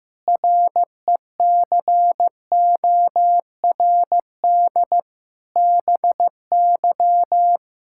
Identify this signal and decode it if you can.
Morse code, heard as a keyed tone, about 15 wpm, 700 Hz. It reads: RECORD BY
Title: Pangram Challenge v2 - 15wpm